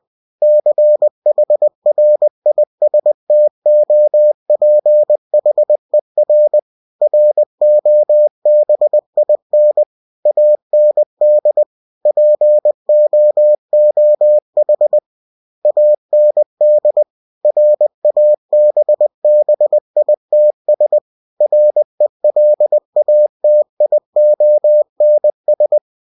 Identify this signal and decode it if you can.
Morse code, heard as a keyed tone, about 20 wpm, 600 Hz. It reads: CHRISTOPHER ROBIN AND POOH AND RABBITS RELATIONS
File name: cw-books / winnie-the-pooh / winnie-20wpm-page-0081